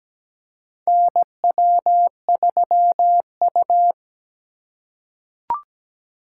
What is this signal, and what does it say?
Morse code, heard as a keyed tone, about 17 wpm, 700 Hz. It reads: NW3U
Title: USA Call Signs - 4 Characters - 3x Character Spacing - 17wpm